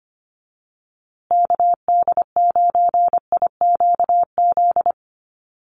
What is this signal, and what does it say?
Morse code, heard as a keyed tone, about 25 wpm, 700 Hz. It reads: KD9IQZ
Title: USA Call Signs Repeated 2x 25wpm